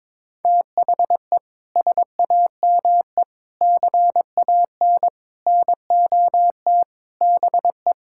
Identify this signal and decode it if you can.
Morse code, heard as a keyed tone, about 22 wpm, 700 Hz. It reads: THE SAME CAN NOT BE
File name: practice / 22wpm / Sets of 7 Words - 22wpm